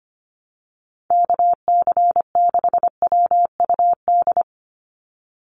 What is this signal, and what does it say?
Morse code, heard as a keyed tone, about 25 wpm, 700 Hz. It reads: KC6WUD
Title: USA Call Signs 25wpm